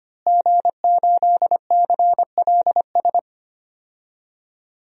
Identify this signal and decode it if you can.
Morse code, heard as a keyed tone, about 25 wpm, 700 Hz. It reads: G8CLS
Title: DXCC-100 Call Signs - 5 Characters - Repeated 3x - 25wpm